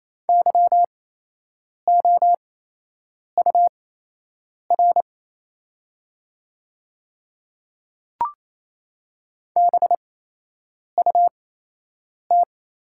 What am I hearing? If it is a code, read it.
Morse code, heard as a keyed tone, about 28 wpm, spaced as for 8 wpm, 690 Hz. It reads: YOUR BUT
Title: 8x Character Spacing - Top 200 Words - 28wpm